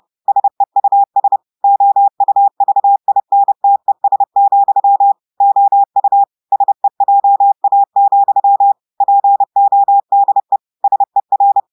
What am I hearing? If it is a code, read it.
Morse code, heard as a keyed tone, about 30 wpm, 800 Hz. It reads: SEUS OUVINTES, OU SEJA, PODE SER
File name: audios / portugais / portugais texte en clair 2 vitesse 30 mots minutes